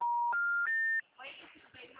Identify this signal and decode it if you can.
Special information tone, with speech.